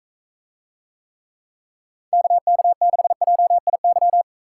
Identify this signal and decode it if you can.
Morse code, heard as a keyed tone, about 44 wpm, 700 Hz. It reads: KK6JIY